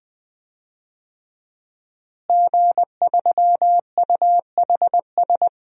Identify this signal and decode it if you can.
Morse code, heard as a keyed tone, about 21 wpm, 700 Hz. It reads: G3UHS
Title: DXCC-100 Call Signs - 5 Characters - Repeated 2x - 20wpm